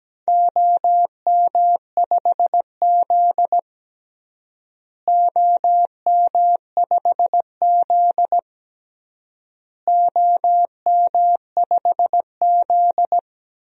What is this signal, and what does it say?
Morse code, heard as a keyed tone, about 17 wpm, 700 Hz. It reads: OM5Z OM5Z OM5Z
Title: DXCC-100 Call Signs - 4 Characters - Repeated 3x - 17wpm